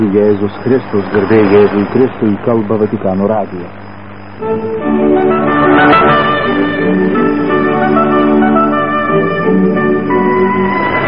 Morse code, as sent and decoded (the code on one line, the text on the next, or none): none